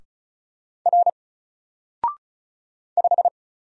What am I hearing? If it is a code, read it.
Morse code, heard as a keyed tone, about 34 wpm, 700 Hz. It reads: R 5
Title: Single Letter-Number Alt 35wpm